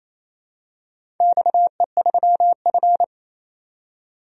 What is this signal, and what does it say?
Morse code, heard as a keyed tone, about 28 wpm, 700 Hz. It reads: XE3F
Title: DXCC-100 Call Signs - 4 Characters - 28wpm